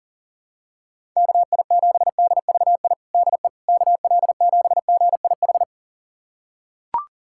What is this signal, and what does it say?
Morse code, heard as a keyed tone, about 41 wpm, 690 Hz. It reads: KI7DVI DE KL7GIH
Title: USA Callsign DE Callsign - 6 Characters - 40wpm